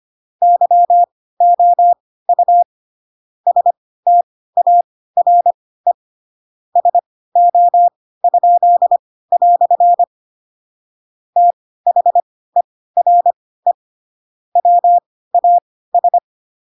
Morse code, heard as a keyed tone, about 25 wpm, spaced as for 16 wpm, 700 Hz. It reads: YOU STARE SO?" THERE WAS
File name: Pinocchio / Pinocchio ch3 10 25 ik1hge